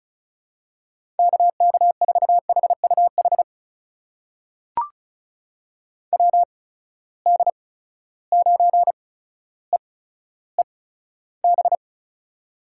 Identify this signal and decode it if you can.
Morse code, heard as a keyed tone, about 36 wpm, spaced as for 15 wpm, 700 Hz. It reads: KK4HUH WD9EEB
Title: USA Call Signs - 6 Characters - 8x Character Spacing - 35wpm